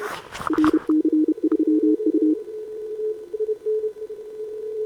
Morse code, heard as a keyed tone, about 31 wpm, 310 Hz. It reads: RC3U
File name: CQWWCW2025 / Sobota 15m ZD7VJ